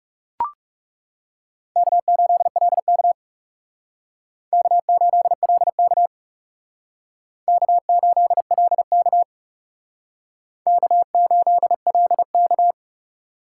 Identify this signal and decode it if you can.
Morse code, heard as a keyed tone, about 37 wpm, 700 Hz. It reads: K8LK K8LK K8LK K8LK
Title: Speed Racing - USA Call Signs 30wpm